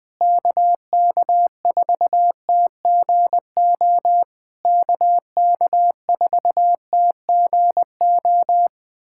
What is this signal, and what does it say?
Morse code, heard as a keyed tone, about 20 wpm, 700 Hz. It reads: KK4TGO KK4TGO
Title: SOTA QSO Practice 20wpm